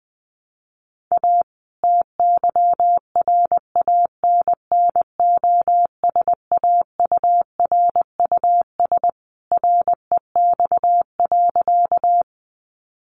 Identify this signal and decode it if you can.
Morse code, heard as a keyed tone, about 20 wpm, 700 Hz. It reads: A TYRANNOSAURUS REX.
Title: A Sound Of Thunder 20wpm